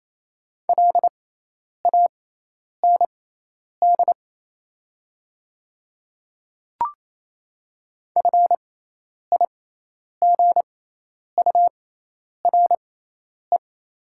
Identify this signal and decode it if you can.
Morse code, heard as a keyed tone, about 28 wpm, spaced as for 10 wpm, 690 Hz. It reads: LAND FIGURE
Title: 6x Character Spacing - Top 500 Words - 28wpm